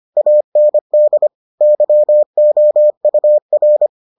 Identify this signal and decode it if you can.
Morse code, heard as a keyed tone, about 25 wpm, 600 Hz. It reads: AND YOUR